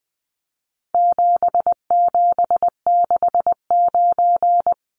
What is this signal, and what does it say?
Morse code, heard as a keyed tone, about 20 wpm, 700 Hz. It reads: 7769
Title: Numbers - 4-Digit 20wpm